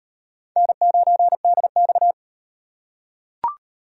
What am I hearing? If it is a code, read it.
Morse code, heard as a keyed tone, about 38 wpm, 700 Hz. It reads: N9DX